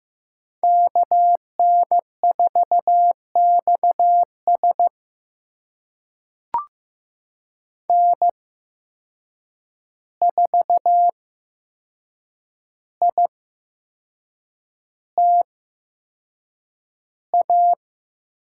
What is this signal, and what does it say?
Morse code, heard as a keyed tone, about 15 wpm, spaced as for 6 wpm, 700 Hz. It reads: KN4XS N4ITA
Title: USA Call Signs - 5 Characters - 8x Character Spacing - 15wpm